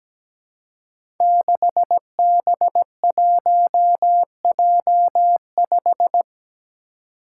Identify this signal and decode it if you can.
Morse code, heard as a keyed tone, about 17 wpm, 700 Hz. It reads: 6B1J5
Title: Five Challenging Letter-Number Combinations 17wpm